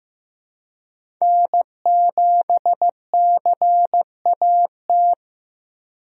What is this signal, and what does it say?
Morse code, heard as a keyed tone, about 15 wpm, 700 Hz. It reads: N7CAT
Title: USA Call Signs - 5 Characters - Repeated 3x - 15wpm